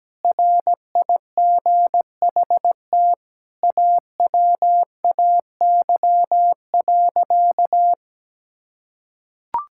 Morse code, heard as a keyed tone, about 17 wpm, 700 Hz. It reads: RIGHT AWAY.
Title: Sentences from Top 1600 Words 17wpm